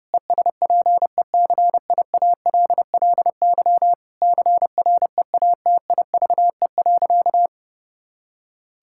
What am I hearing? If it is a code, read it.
Morse code, heard as a keyed tone, about 30 wpm, 700 Hz. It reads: ESPECIALLY CREATIVE.
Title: Sentences from Top 1000 Words - Review Part 5 - 30wpm